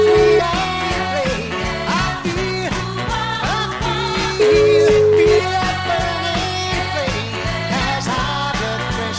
Ringback tone (Europe, 425 Hz), with noise.